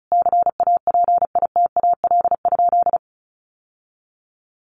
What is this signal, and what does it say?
Morse code, heard as a keyed tone, about 34 wpm, 700 Hz. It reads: CAPITAL?
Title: Sentences from Top 1100 Words 35wpm